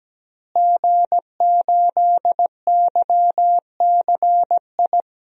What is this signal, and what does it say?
Morse code, heard as a keyed tone, about 17 wpm, 700 Hz. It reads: G8YCI
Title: DXCC-100 Call Signs - 5 Characters - Repeated 2x - 17wpm